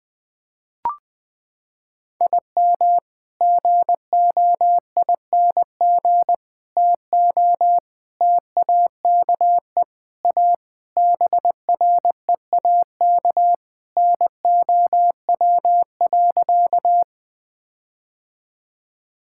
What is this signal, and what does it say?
Morse code, heard as a keyed tone, about 21 wpm, 700 Hz. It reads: IM GOING TO TAKE A BREAK NOW.
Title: Sentences with 5-Word Phrases - 20wpm